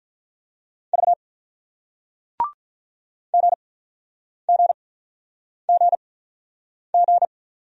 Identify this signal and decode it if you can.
Morse code, heard as a keyed tone, about 48 wpm, spaced as for 9 wpm, 700 Hz. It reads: V GGGG